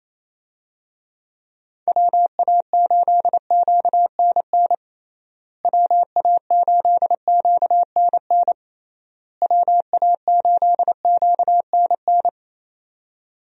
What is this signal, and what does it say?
Morse code, heard as a keyed tone, about 28 wpm, 700 Hz. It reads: WA8QNN WA8QNN WA8QNN
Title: USA Call Signs - 6 Characters - Repeated 3x - 28wpm